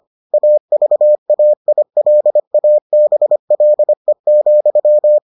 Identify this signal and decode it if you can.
Morse code, heard as a keyed tone, about 25 wpm, 600 Hz. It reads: AVAILABLE,